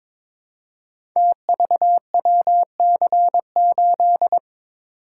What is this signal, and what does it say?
Morse code, heard as a keyed tone, about 22 wpm, 700 Hz. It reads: TVWC8
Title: Code Groups - 5 Letter:Numbers - 22wpm